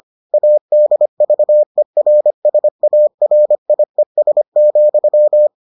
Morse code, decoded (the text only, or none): ADVERSARIES,